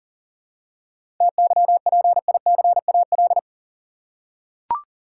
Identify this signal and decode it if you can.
Morse code, heard as a keyed tone, about 40 wpm, 700 Hz. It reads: TYPICAL
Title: Seven Letter Words - 40wpm